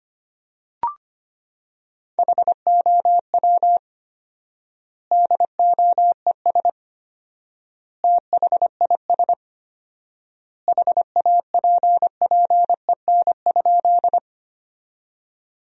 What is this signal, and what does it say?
Morse code, heard as a keyed tone, about 25 wpm, 700 Hz. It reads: HOW DOES THIS HAPPEN?
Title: 4x Word-Spacing - Sentences from Top 400 Words - 25wpm